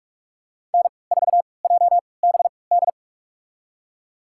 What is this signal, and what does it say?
Morse code, heard as a keyed tone, about 44 wpm, spaced as for 25 wpm, 700 Hz. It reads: N4JBD
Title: USA Call Signs - 3x Character Spacing - 45wpm